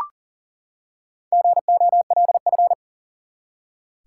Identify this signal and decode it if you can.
Morse code, heard as a keyed tone, about 38 wpm, 700 Hz. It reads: GOLF
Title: Four Letter Words - 40wpm